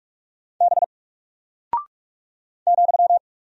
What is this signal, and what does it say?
Morse code, heard as a keyed tone, about 46 wpm, 700 Hz. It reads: B ,
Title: Single Character (Letter:Number:Punctuation:Prosign) Alt 45wpm